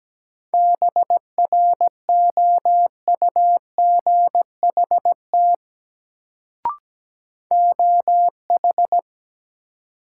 Morse code, heard as a keyed tone, about 17 wpm, 700 Hz. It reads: BROUGHT OH
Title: Sentences from Top 500 Words - New Word Drill - 17wpm